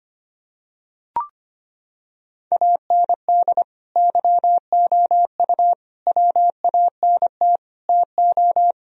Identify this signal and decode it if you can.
Morse code, heard as a keyed tone, about 25 wpm, 700 Hz. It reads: AND YOU WANT TO
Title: Sentences from Top 1000 Words - Review Part 4 - 25wpm